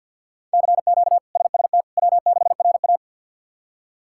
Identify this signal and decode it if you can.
Morse code, heard as a keyed tone, about 53 wpm, 700 Hz. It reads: CQ SST W6RA
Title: K1USN SST QSO Practice - 50wpm